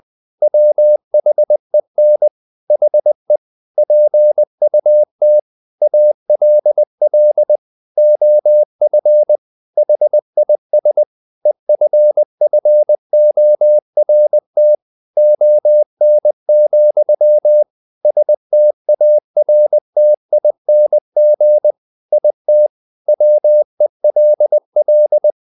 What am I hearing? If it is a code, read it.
Morse code, heard as a keyed tone, about 20 wpm, 600 Hz. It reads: WHEN HE PUT ALL OF HIS EFFORT ON, STARTING IT WELL
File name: the old man and the sea part 2 20wpm